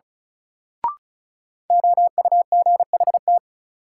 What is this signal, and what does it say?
Morse code, heard as a keyed tone, about 36 wpm, 700 Hz. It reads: OUGHT
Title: Sets of 5 Words Encore - New Word Drill - 35wpm